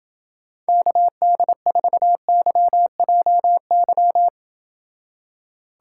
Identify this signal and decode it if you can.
Morse code, heard as a keyed tone, about 29 wpm, 700 Hz. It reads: KD4YJY